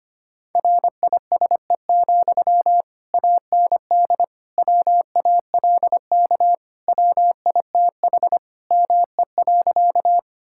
Morse code, heard as a keyed tone, about 25 wpm, 700 Hz. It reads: RISE, AND WALK WITH ME.
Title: Sentences from Top 1000 Words - Review Part 6 - 25wpm